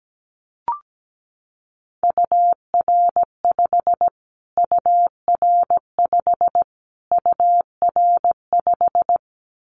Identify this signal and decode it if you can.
Morse code, heard as a keyed tone, about 17 wpm, 700 Hz. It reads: UR5 UR5 UR5
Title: Callsign Prefixes for Top 25 DXCC - Repeated 3x - 17wpm